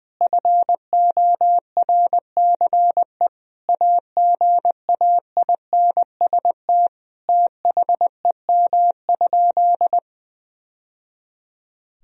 Morse code, heard as a keyed tone, about 20 wpm, 700 Hz. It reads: FORCE AGAINST THEM?
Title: Sentences from Top 1000 Words - Review Part 4 - 20wpm